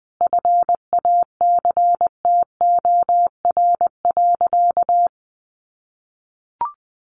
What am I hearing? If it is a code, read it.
Morse code, heard as a keyed tone, about 19 wpm, 700 Hz. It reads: FACTOR.
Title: Sentences from Top 1300 Words 20wpm